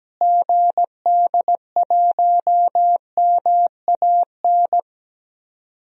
Morse code, heard as a keyed tone, about 17 wpm, 700 Hz. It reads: GD1MAN
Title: DXCC-100 Call Signs - 6 Characters - Repeated 3x - 17wpm